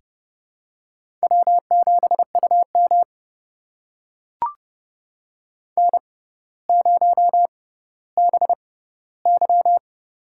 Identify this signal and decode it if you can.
Morse code, heard as a keyed tone, about 29 wpm, spaced as for 15 wpm, 700 Hz. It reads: W7UM N0BY
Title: USA Call Signs - 4 Characters - 6x Character Spacing - 30wpm